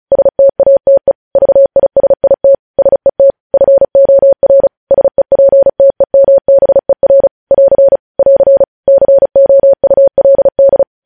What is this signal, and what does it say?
Morse code, heard as a keyed tone, about 36 wpm, 550 Hz. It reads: STATE VISIT SET FOR SEPTEMBER <AR> <AR> COULD
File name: podcast / News headlines-1x-35x35